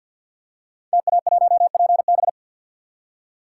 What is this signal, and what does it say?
Morse code, heard as a keyed tone, about 52 wpm, 700 Hz. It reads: TA1PB
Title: Code Groups - 5 Letter:Numbers - 50wpm